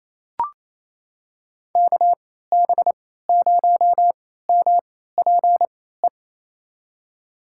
Morse code, heard as a keyed tone, about 27 wpm, spaced as for 16 wpm, 700 Hz. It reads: KB0MPE
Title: USA Call Signs - 3x Character Spacing - 28wpm